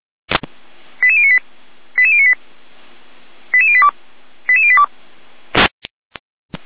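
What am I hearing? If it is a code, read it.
Five-tone selective call (ZVEI), heard as quick runs of five tones, tones of about 70 ms. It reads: 88098 88098 88082 88082
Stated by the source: ZVEI five-tone selective call received on an SDR.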